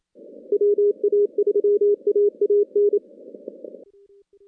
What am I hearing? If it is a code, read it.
Morse code, heard as a keyed tone, about 28 wpm, 410 Hz. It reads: WA3AAN
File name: contests / CW-SS-k5tr- -2100